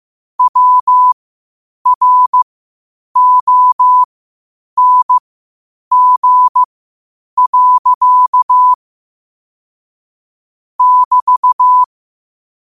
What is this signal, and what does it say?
Morse code, heard as a keyed tone, about 15 wpm, spaced as for 8 wpm, 1000 Hz. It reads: WRONG. <BT>
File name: qotdfiles / 8wpm